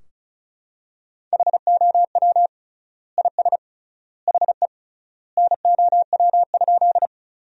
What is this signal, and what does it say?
Morse code, heard as a keyed tone, about 36 wpm, 700 Hz. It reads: HOW IS HE NOW?
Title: 3x Word-Spacing - Sentences from Top 100 Words - 35wpm